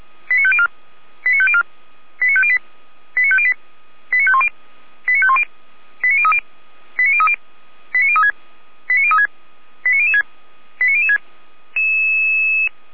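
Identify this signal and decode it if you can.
Five-tone selective call (ZVEI), heard as quick runs of five tones, tones of about 70 ms. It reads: 89504 89504 89508 89508 89510 89510 89030 89030 89036 89036 89006 89006
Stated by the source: ZVEI five-tone selective call received by SDR.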